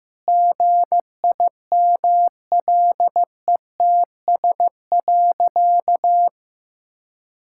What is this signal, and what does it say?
Morse code, heard as a keyed tone, about 15 wpm, 700 Hz. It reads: GIMLETS.
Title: Pangram Challenge v2 - 15wpm